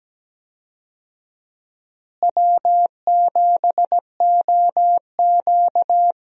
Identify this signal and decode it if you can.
Morse code, heard as a keyed tone, about 17 wpm, 700 Hz. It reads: W7OQ
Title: USA Call Signs - 4 Characters - Repeated 3x - 17wpm